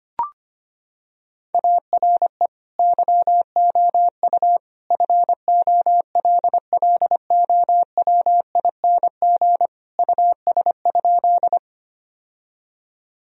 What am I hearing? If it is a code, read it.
Morse code, heard as a keyed tone, about 25 wpm, 700 Hz. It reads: ARE YOU FOLLOWING US?
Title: Sentences from Top 1000 Words - Review Part 4 - 25wpm